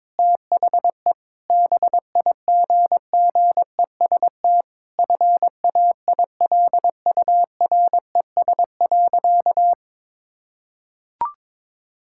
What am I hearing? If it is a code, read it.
Morse code, heard as a keyed tone, about 22 wpm, 700 Hz. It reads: THE BIGGEST FAILURES.
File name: Sentences with 7-Word Phrases - 22wpm